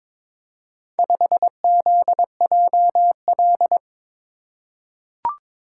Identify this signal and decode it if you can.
Morse code, heard as a keyed tone, about 23 wpm, 700 Hz. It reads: 5ZJL
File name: practice / 22wpm / Four Letter-Number Combinations - 22wpm